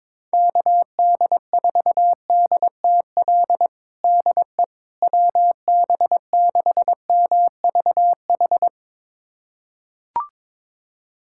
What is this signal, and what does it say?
Morse code, heard as a keyed tone, about 22 wpm, 700 Hz. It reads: KD4DTL DE WB6MVH
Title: USA Callsign DE Callsign - 6 Characters - 22wpm